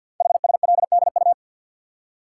Morse code, heard as a keyed tone, about 53 wpm, 700 Hz. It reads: HSLDU
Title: Code Groups - 5 Letter:Numbers - 50wpm